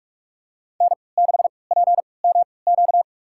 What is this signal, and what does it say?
Morse code, heard as a keyed tone, about 46 wpm, spaced as for 25 wpm, 690 Hz. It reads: N6PMQ
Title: USA Call Signs - 3x Character Spacing - 45wpm